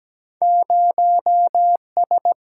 Morse code, heard as a keyed tone, about 17 wpm, 700 Hz. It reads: 0 S
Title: Single-Letter Single-Number 17wpm